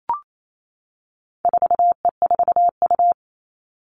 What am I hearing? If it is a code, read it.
Morse code, heard as a keyed tone, about 28 wpm, 700 Hz. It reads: 4E4U
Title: All Cut Numbers - 4-Digit 28wpm